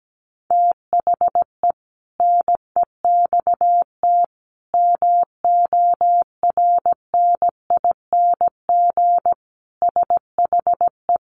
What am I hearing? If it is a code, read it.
Morse code, heard as a keyed tone, about 17 wpm, 700 Hz. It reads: THE NEXT MORNING SHE
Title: Sentences from Top 1200 Words 17wpm